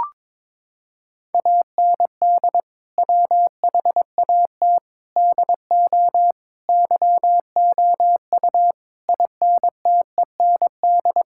Morse code, heard as a keyed tone, about 22 wpm, 700 Hz. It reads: AND WHAT DO YOU INTEND
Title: Sentences from Top 1600 Words 22wpm